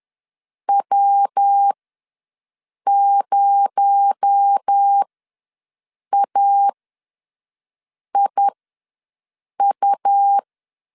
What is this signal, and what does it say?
Morse code, heard as a keyed tone, about 10 wpm, spaced as for 5 wpm, 800 Hz. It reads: W0AIU
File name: LEZIONI / 6 PKX0 / 6 PKX0 ws 10wpm 4220